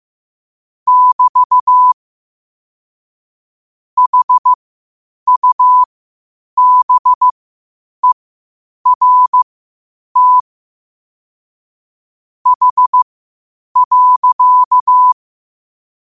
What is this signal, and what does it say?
Morse code, heard as a keyed tone, about 15 wpm, spaced as for 8 wpm, 1000 Hz. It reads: <BT> HUBERT H.
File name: qotdfiles / 8wpm